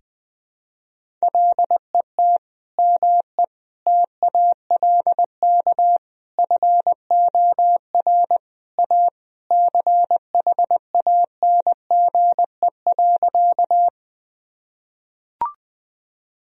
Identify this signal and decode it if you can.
Morse code, heard as a keyed tone, about 20 wpm, 700 Hz. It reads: LET ME TALK FOR A CHANGE.
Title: Sentences from Top 1000 Words - Review Part 1 - 20wpm